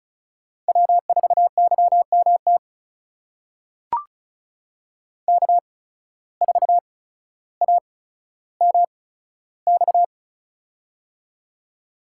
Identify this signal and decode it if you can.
Morse code, heard as a keyed tone, about 36 wpm, spaced as for 15 wpm, 700 Hz. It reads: W4YMT K4AMX
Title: USA Call Signs - 5 Characters - 8x Character Spacing - 35wpm